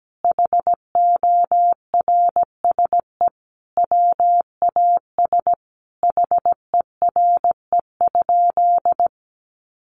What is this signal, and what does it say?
Morse code, heard as a keyed tone, about 17 wpm, 700 Hz. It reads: HORSE WAS HERE?
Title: Sentences from Top 1500 Words 17wpm